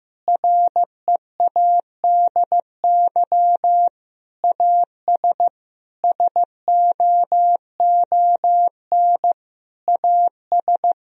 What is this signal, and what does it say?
Morse code, heard as a keyed tone, about 15 wpm, 700 Hz. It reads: READY AS SOON AS
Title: Sentences from Top 1000 Words - Review Part 2 - 15wpm